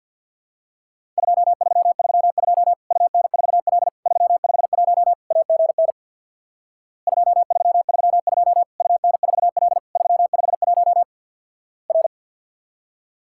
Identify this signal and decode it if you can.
Morse code, heard as a keyed tone, about 50 wpm, 700 Hz. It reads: 2332 UN4L 351 AGN 2332 UN4L 351 R
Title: Worked All Europe DX Contest - QTC Exchange with Repeats - 50wpm